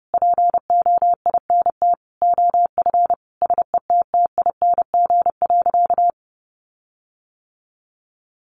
Morse code, heard as a keyed tone, about 30 wpm, 700 Hz. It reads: POINT OF SETTING.
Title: Sentences from Top 1300 Words 30wpm